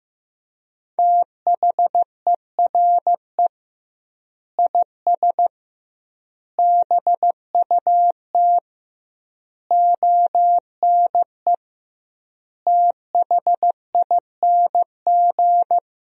Morse code, heard as a keyed tone, about 14 wpm, 700 Hz. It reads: THERE IS BUT ONE THING
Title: 2x Word-Spacing - Sentences from Top 100 Words - 15wpm